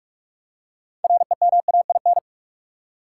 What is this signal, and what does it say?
Morse code, heard as a keyed tone, about 43 wpm, 700 Hz. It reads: REMAIN
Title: Six Letter Words - 45wpm